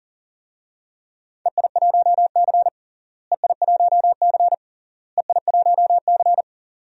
Morse code, heard as a keyed tone, about 40 wpm, 700 Hz. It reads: EI1C EI1C EI1C
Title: DXCC-100 Call Signs - 4 Characters - Repeated 3x - 40wpm